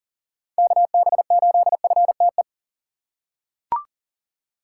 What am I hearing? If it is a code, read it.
Morse code, heard as a keyed tone, about 40 wpm, 700 Hz. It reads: KB8FTE
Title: USA Call Signs - 6 Characters - 6x Character Spacing - 40wpm